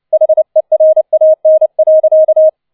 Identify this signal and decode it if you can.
Morse code, heard as a keyed tone, about 29 wpm, 630 Hz. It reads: HERAN.